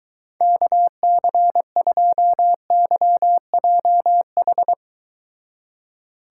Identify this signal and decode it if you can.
Morse code, heard as a keyed tone, about 24 wpm, 700 Hz. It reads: KC2YJH